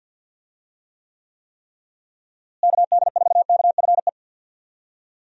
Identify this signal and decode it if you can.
Morse code, heard as a keyed tone, about 51 wpm, 700 Hz. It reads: KD4KFE
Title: USA Call Signs - 6 Characters - Repeated 3x - 50wpm